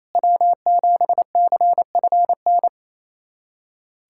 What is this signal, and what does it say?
Morse code, heard as a keyed tone, about 28 wpm, 700 Hz. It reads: W7CFN
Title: USA Callsign DE Callsign - 5 Characters - 28wpm